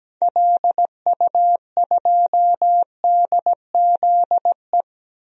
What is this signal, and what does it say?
Morse code, heard as a keyed tone, about 17 wpm, 700 Hz. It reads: LU2DZE
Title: DXCC-100 Call Signs - 6 Characters - 17wpm